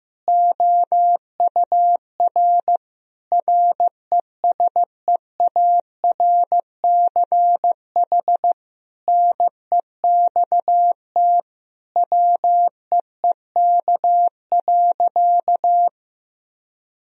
Morse code, heard as a keyed tone, about 16 wpm, 700 Hz. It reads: OUR RESEARCH NEXT WEEK.
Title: Sentences from Top 1000 Words - Review Part 4 - 15wpm